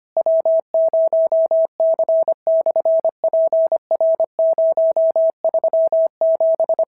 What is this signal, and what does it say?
Morse code, heard as a keyed tone, about 25 wpm, 650 Hz. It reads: W0C/PR037
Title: SOTA QSO Practice 25wpm